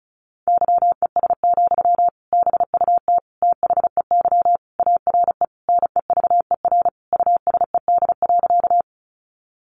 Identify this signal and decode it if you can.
Morse code, heard as a keyed tone, about 34 wpm, 700 Hz. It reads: YES, BUT THEY ARE NEVER USED.
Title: Sentences from Top 600 Words 35wpm